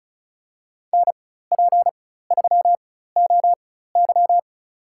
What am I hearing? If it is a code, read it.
Morse code, heard as a keyed tone, about 36 wpm, spaced as for 16 wpm, 700 Hz. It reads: NP3OY
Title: USA Call Signs - 5 Characters - 4x Character Spacing - 35wpm